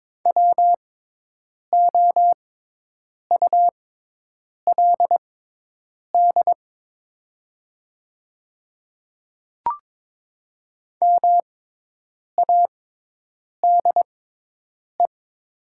Morse code, heard as a keyed tone, about 22 wpm, spaced as for 8 wpm, 700 Hz. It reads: WOULD MADE
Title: 6x Character Spacing - Top 200 Words - 22wpm